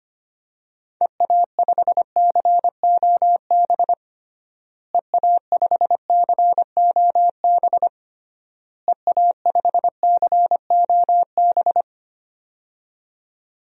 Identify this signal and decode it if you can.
Morse code, heard as a keyed tone, about 25 wpm, 700 Hz. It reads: EA5COB EA5COB EA5COB
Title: DXCC-100 Call Signs - 6 Characters - Repeated 3x - 25wpm